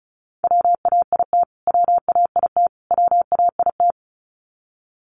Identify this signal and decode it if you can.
Morse code, heard as a keyed tone, about 34 wpm, 700 Hz. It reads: WAIT WAIT WAIT
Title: Top 500 Words Repeated 35wpm